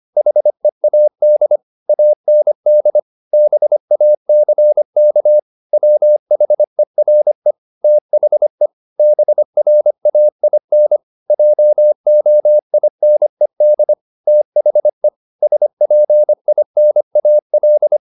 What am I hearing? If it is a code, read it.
Morse code, heard as a keyed tone, about 25 wpm, 600 Hz. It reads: HEAD AND BACK WHERE THE BRAIN JOINED THE SPINAL